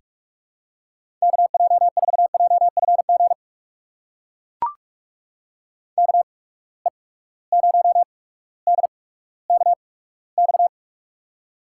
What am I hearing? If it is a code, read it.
Morse code, heard as a keyed tone, about 44 wpm, spaced as for 19 wpm, 700 Hz. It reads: KJ4JFG KE0DKX
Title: USA Call Signs - 6 Characters - 8x Character Spacing - 45wpm